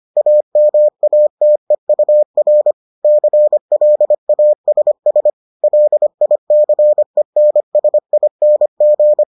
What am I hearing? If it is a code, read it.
Morse code, heard as a keyed tone, about 25 wpm, 600 Hz. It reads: AMATEUR CLASS LICENSING